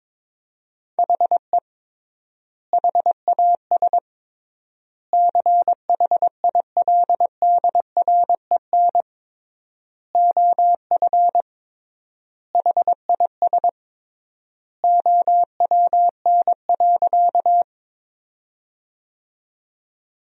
Morse code, heard as a keyed tone, about 22 wpm, 700 Hz. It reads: HE HAS CHILDREN OF HIS OWN.
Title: 3x Word-Spacing - Sentences from Top 300 Words - 22wpm